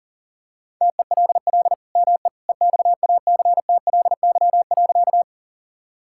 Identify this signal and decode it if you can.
Morse code, heard as a keyed tone, about 40 wpm, 700 Hz. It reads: TELL ME EXACTLY.
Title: Sentences from Top 1000 Words - Review Part 6 - 40wpm